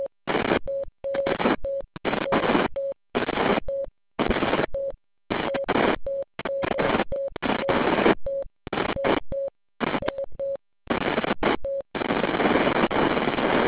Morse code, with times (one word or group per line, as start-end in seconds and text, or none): none